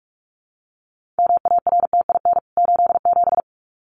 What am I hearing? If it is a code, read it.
Morse code, heard as a keyed tone, about 44 wpm, 700 Hz. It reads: MARTIN 87
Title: CWT Exchange 45wpm